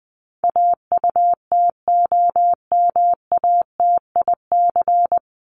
Morse code, heard as a keyed tone, about 20 wpm, 700 Hz. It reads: AUTOMATIC
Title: Top 1751-2000 Words 20wpm